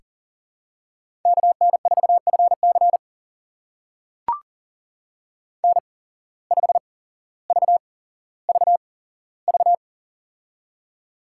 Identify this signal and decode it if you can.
Morse code, heard as a keyed tone, about 40 wpm, spaced as for 17 wpm, 700 Hz. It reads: KN4FC N5VVV
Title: USA Call Signs - 5 Characters - 8x Character Spacing - 40wpm